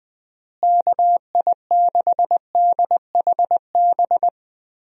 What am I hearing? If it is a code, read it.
Morse code, heard as a keyed tone, about 20 wpm, 700 Hz. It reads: KI6DHB
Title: USA Call Signs - 6 Characters - Repeated 3x - 20wpm